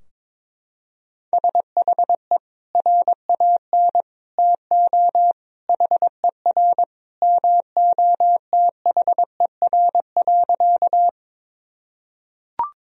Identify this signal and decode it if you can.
Morse code, heard as a keyed tone, about 22 wpm, 700 Hz. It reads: SHE RAN TO HER MOTHER.
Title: Sentences from Top 1000 Words - Review Part 3 - 22wpm